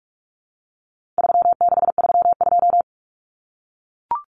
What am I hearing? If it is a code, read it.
Morse code, heard as a keyed tone, about 44 wpm, 700 Hz. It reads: 3632
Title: Common Cut Numbers - 4-Digit 45wpm